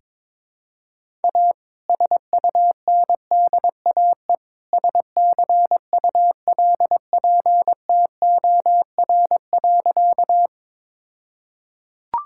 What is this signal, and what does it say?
Morse code, heard as a keyed tone, about 21 wpm, 700 Hz. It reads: A SUNDAE SCULPTOR.
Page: morse-code-ninja-practice-22wpm